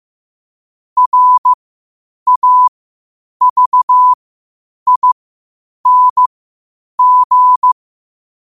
Morse code, heard as a keyed tone, about 15 wpm, spaced as for 8 wpm, 1000 Hz. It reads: RAVING
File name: qotdfiles / 8wpm